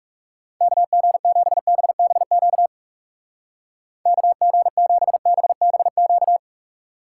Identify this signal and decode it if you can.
Morse code, heard as a keyed tone, about 35 wpm, 700 Hz. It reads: KG7BBQ KG7BBQ
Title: Speed Racing - USA 2x3 Call Signs 30wpm